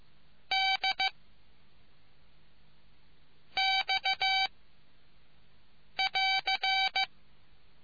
Morse code, heard as a keyed tone, about 15 wpm, 770 Hz. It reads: D X <AR>